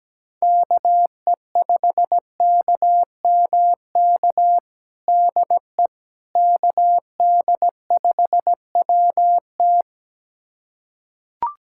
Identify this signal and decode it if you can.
Morse code, heard as a keyed tone, about 17 wpm, 700 Hz. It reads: KE5KMK DE KD5WT
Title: USA Callsign DE Callsign 17wpm